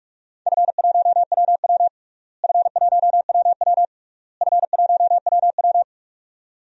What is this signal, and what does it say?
Morse code, heard as a keyed tone, about 46 wpm, 700 Hz. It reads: F1WW F1WW F1WW
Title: DXCC-100 Call Signs - 4 Characters - Repeated 3x - 45wpm